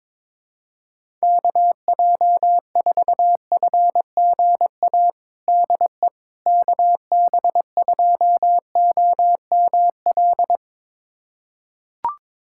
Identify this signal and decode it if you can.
Morse code, heard as a keyed tone, about 21 wpm, 700 Hz. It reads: KJ4FGA DE KB2OML
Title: USA Callsign DE Callsign - 6 Characters - 22wpm